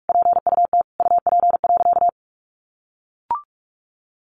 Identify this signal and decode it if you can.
Morse code, heard as a keyed tone, about 44 wpm, 700 Hz. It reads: PUT UP.
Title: A Sound Of Thunder 45wpm